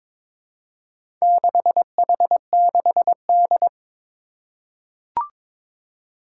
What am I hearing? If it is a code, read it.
Morse code, heard as a keyed tone, about 22 wpm, 700 Hz. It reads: 6H6D
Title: Four Challenging Letter-Number Combinations 22wpm